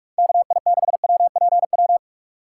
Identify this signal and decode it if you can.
Morse code, heard as a keyed tone, about 42 wpm, 700 Hz. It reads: KI6WPW
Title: USA Call Signs - 2x Character Spacing - 45wpm